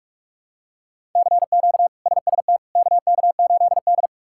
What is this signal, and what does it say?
Morse code, heard as a keyed tone, about 46 wpm, 700 Hz. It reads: CQ SST KK8D
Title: K1USN SST QSO Practice - 45wpm